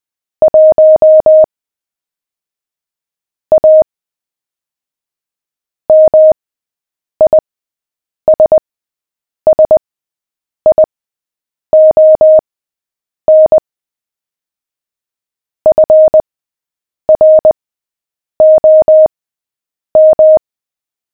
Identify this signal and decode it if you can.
Morse code, heard as a keyed tone, about 20 wpm, spaced as for 8 wpm, 620 Hz. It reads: 1 A MISSION FROM
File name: cw / fists / Short-Story-7-8wpm